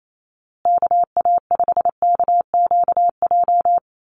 Morse code, heard as a keyed tone, about 28 wpm, 700 Hz. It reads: KA5KQJ
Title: USA Call Signs Repeated 2x 28wpm